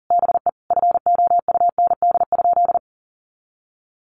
Morse code, heard as a keyed tone, about 40 wpm, 700 Hz. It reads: BE FOUND?
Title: Sentences from Top 300 Words 40wpm